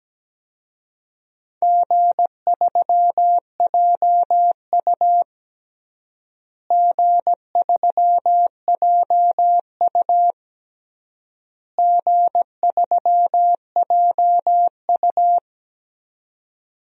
Morse code, heard as a keyed tone, about 17 wpm, 700 Hz. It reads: G3JU G3JU G3JU
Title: DXCC-100 Call Signs - 4 Characters - Repeated 3x - 17wpm